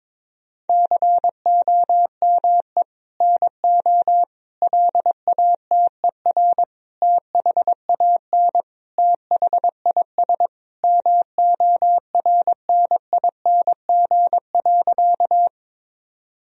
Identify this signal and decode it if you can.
Morse code, heard as a keyed tone, about 22 wpm, 700 Hz. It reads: COME NO LATER THAN THIS MORNING.
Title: Sentences from Top 1000 Words - Review Part 4 - 22wpm